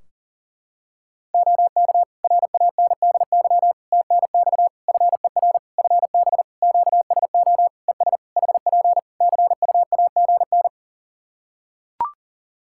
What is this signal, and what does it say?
Morse code, heard as a keyed tone, about 39 wpm, 700 Hz. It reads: OK RANDY TNX FER FB QSO ES HP CUAGN
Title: Practice QSO - Exchange 4 - Ending - 40wpm